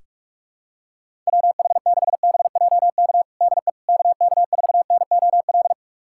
Morse code, heard as a keyed tone, about 46 wpm, 690 Hz. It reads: WH6BJK DE KK4NOL
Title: USA Callsign DE Callsign 45wpm